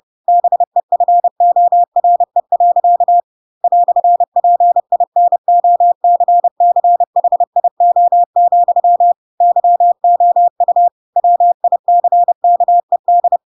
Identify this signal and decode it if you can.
Morse code, heard as a keyed tone, about 30 wpm, 700 Hz. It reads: BEFORE. "PINOCCHIO, YOU WICKED
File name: Pinocchio / Pinocchio ch3 30 30 ik1hge